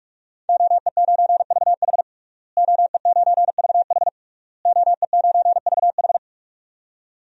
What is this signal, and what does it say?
Morse code, heard as a keyed tone, about 46 wpm, 700 Hz. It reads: OE9VH OE9VH OE9VH
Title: DXCC-100 Call Signs - 5 Characters - Repeated 3x - 45wpm